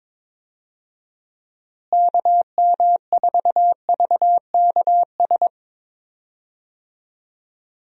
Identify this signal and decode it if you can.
Morse code, heard as a keyed tone, about 22 wpm, 700 Hz. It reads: KM4VKS
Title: USA Call Signs - 6 Characters - Repeated 2x - 22wpm